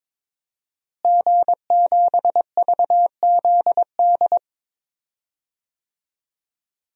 Morse code, heard as a keyed tone, about 22 wpm, 700 Hz. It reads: G7VZD
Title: DXCC-100 Call Signs - 5 Characters - Repeated 3x - 22wpm